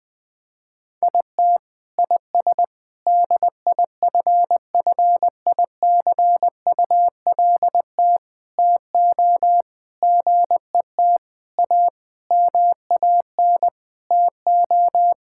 Morse code, heard as a keyed tone, about 21 wpm, 700 Hz. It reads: IT IS DIFFICULT TO GET A MAN TO
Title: Sentences with 9-Word Phrases - Spoken Beginning - 20wpm